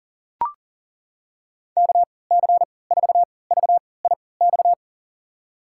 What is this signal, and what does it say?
Morse code, heard as a keyed tone, about 40 wpm, spaced as for 23 wpm, 700 Hz. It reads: KC4VIX
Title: USA Call Signs - 6 Characters - 3x Character Spacing - 40wpm